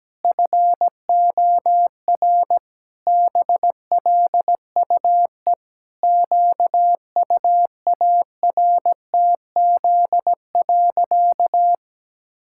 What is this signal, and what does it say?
Morse code, heard as a keyed tone, about 17 wpm, 700 Hz. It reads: FOR BLUE QUARTZ.
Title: Pangram Challenge v2 - 17wpm